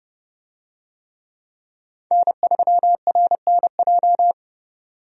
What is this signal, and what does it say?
Morse code, heard as a keyed tone, about 32 wpm, 700 Hz. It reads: N3RNJ
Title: USA Call Signs - 5 Characters - 30wpm